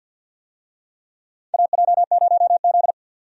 Morse code, heard as a keyed tone, about 46 wpm, 700 Hz. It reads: AJ0Z
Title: USA Call Signs - 4 Characters - 50wpm